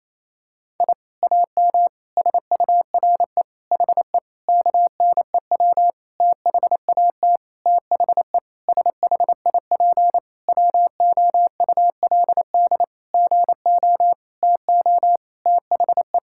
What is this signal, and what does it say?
Morse code, heard as a keyed tone, about 28 wpm, 700 Hz. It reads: I AM SURE HE KNEW THAT THE SHIP WOULD GO TO THE
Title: Sentences from Top 1000 Words - Review Part 6 - 28wpm